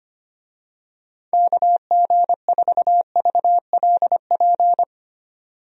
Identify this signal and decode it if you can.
Morse code, heard as a keyed tone, about 25 wpm, 700 Hz. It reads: KG4VLP
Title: USA Call Signs - 6 Characters - 25wpm